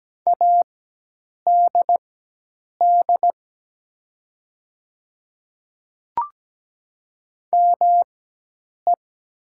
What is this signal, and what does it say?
Morse code, heard as a keyed tone, about 17 wpm, spaced as for 8 wpm, 700 Hz. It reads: ADD ME
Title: 4x Character Spacing - Top 200 Words - 17wpm